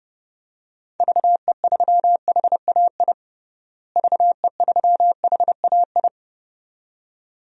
Morse code, heard as a keyed tone, about 30 wpm, 700 Hz. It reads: VE3HAI VE3HAI
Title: DXCC-100 Call Signs - 6 Characters - Repeated 2x - 30wpm